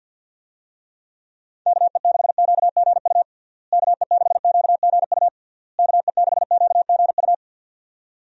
Morse code, heard as a keyed tone, about 51 wpm, 700 Hz. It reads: KE6QGU KE6QGU KE6QGU
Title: USA Call Signs - 6 Characters - Repeated 3x - 50wpm